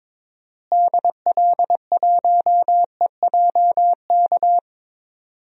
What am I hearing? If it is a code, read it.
Morse code, heard as a keyed tone, about 22 wpm, 700 Hz. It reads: DL1EJK
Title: DXCC-100 Call Signs - 6 Characters - Repeated 3x - 22wpm